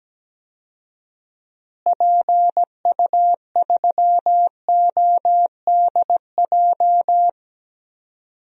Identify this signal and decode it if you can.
Morse code, heard as a keyed tone, about 17 wpm, 700 Hz. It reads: PU3ODJ
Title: DXCC-100 Call Signs - 6 Characters - Repeated 3x - 17wpm